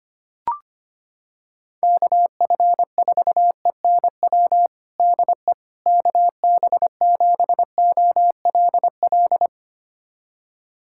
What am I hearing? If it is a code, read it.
Morse code, heard as a keyed tone, about 25 wpm, 700 Hz. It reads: KF4ENW DE KB7OLL
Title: USA Callsign DE Callsign - 6 Characters - 25wpm